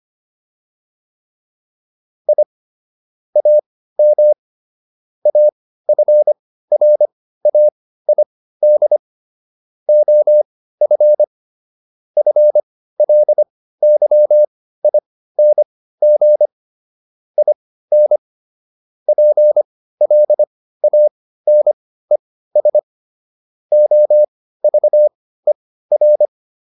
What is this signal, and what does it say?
Morse code, heard as a keyed tone, about 25 wpm, spaced as for 15 wpm, 600 Hz. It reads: I AM AFRAID OF FLYING IN PLANES OVER